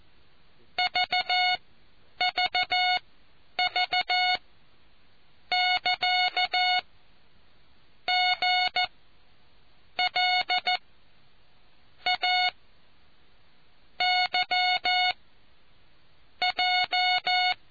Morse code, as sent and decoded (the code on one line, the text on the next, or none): ...- ...- ...- -.-.- --. .-.. .- / -.-- .---
VVV<KA>GLA YJ